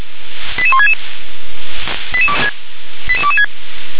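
Five-tone selective call (ZVEI), heard as quick runs of five tones, tones of about 70 ms. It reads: 88166 88226
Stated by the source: ZVEI five-tone selective call received by SDR.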